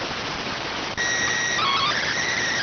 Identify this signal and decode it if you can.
SSTV (slow-scan television), PD 290.